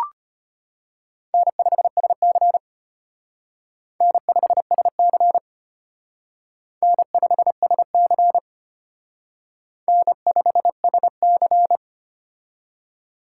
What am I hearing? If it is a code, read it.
Morse code, heard as a keyed tone, about 31 wpm, 700 Hz. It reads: N5SC N5SC N5SC N5SC
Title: Speed Racing - USA 1x2 Call Signs 25wpm